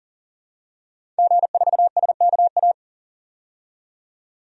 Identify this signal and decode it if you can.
Morse code, heard as a keyed tone, about 39 wpm, 700 Hz. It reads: G4SKA
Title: DXCC-100 Call Signs - 5 Characters - 40wpm